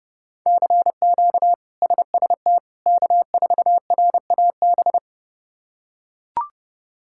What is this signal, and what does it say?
Morse code, heard as a keyed tone, about 30 wpm, 700 Hz. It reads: CQ SST K4RAB
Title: K1USN SST QSO Practice - 30wpm